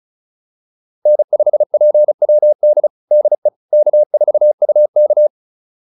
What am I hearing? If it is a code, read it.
Morse code, heard as a keyed tone, about 34 wpm, 600 Hz. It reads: N5PWD DE K4UK
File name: QSO208 35